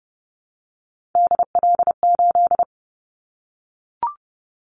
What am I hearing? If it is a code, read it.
Morse code, heard as a keyed tone, about 30 wpm, 700 Hz. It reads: DL8
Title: Callsign Prefixes for Top 25 DXCC - 30wpm